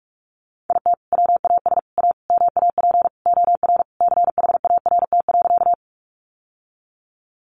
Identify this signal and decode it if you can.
Morse code, heard as a keyed tone, about 44 wpm, 700 Hz. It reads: IT WAS A MAP OR CHART.
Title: Sentences from Top 1300 Words 45wpm